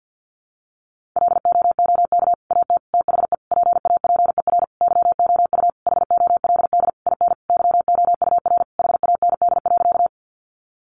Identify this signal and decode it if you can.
Morse code, heard as a keyed tone, about 50 wpm, 700 Hz. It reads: LOOK AT THE PAPER YOU HOLD IN YOUR HAND.
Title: Sentences from Top 400 Words 50wpm